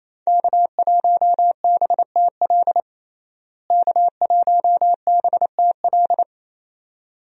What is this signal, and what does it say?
Morse code, heard as a keyed tone, about 28 wpm, 700 Hz. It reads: K1BTL K1BTL
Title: USA Call Signs - 5 Characters - Repeated 2x - 28wpm